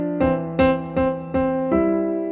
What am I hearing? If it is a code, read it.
There is no code.